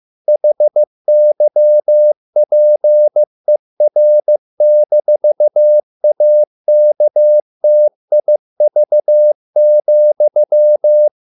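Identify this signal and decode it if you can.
Morse code, heard as a keyed tone, about 15 wpm, 600 Hz. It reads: HYPER-AKTIV,
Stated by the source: Morse code at a slow 15 wpm from CW med Gnister.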